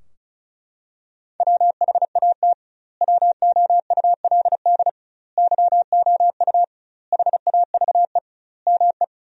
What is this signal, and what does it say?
Morse code, heard as a keyed tone, about 36 wpm, 700 Hz. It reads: WHAT WOULD YOU HAVE ME
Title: 2x Word-Spacing - Sentences from Top 300 Words - 35wpm